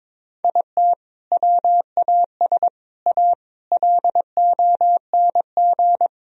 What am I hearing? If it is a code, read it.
Morse code, heard as a keyed tone, about 22 wpm, 700 Hz. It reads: IT WAS A LONG
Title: Sentences from Top 1000 Words - Review Part 4 - 22wpm